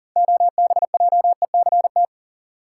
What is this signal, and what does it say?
Morse code, heard as a keyed tone, about 40 wpm, 700 Hz. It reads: OBJECT
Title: Six Letter Words - 40wpm